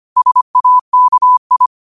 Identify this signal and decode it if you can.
Morse code, heard as a keyed tone, about 25 wpm, 1000 Hz. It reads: SAKI
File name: qotdfiles / 25wpm